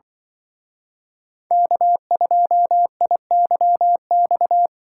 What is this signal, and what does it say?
Morse code, heard as a keyed tone, about 24 wpm, 700 Hz. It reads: K2IYX
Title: Speed Racing - USA 1x3 Call Signs 20wpm